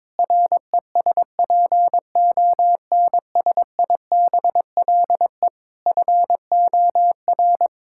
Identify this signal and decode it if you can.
Morse code, heard as a keyed tone, about 22 wpm, 700 Hz. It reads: RESPONSIBLE FOR
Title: Sentences from Top 1600 Words 22wpm